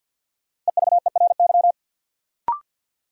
Morse code, heard as a keyed tone, about 53 wpm, 700 Hz. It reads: EVERY
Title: Sets of 5 Words - New Word Drill - 50wpm